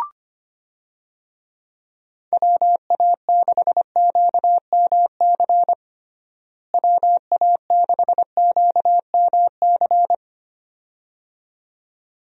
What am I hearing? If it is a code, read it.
Morse code, heard as a keyed tone, about 25 wpm, 700 Hz. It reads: WA6QMC WA6QMC
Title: USA Call Signs - 6 Characters - Repeated 2x - 25wpm